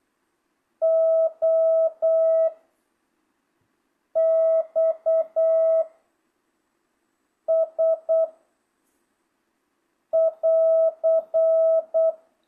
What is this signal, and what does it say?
Morse code, heard as a keyed tone, about 8 wpm, 660 Hz. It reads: O X S <AR>